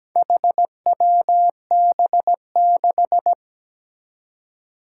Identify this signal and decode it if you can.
Morse code, heard as a keyed tone, about 17 wpm, 700 Hz. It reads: HWB6
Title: Four Challenging Letter-Number Combinations 17wpm